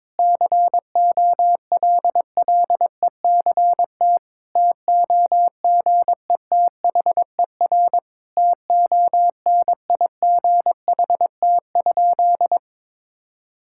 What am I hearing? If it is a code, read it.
Morse code, heard as a keyed tone, about 22 wpm, 700 Hz. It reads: COLLECT TOGETHER TONIGHT?
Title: Sentences from Top 1600 Words 22wpm